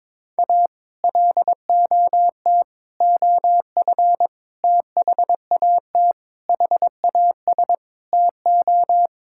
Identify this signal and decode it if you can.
Morse code, heard as a keyed tone, about 22 wpm, 700 Hz. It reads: A LOT OF THAT HAS TO
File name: practice / 22wpm / Sentences with 8-Word Phrases - 22wpm